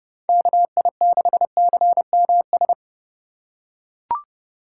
Morse code, heard as a keyed tone, about 32 wpm, 700 Hz. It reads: KI6CMS
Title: USA Call Signs - 6 Characters - 2x Character Spacing - 30wpm